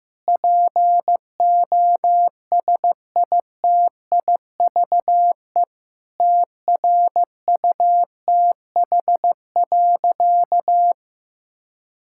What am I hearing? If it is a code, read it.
Morse code, heard as a keyed tone, about 15 wpm, 700 Hz. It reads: POSITIVE TRUTH.
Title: Sentences from Top 1000 Words - Review Part 5 - 15wpm